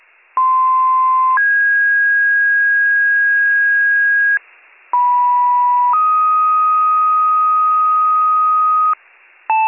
Two-tone page, heard as a one-second tone, then a three-second tone, about 1040 then 1650 Hz, about 980 then 1250 Hz, with noise. It is heard from 0.4 to 8.9 s.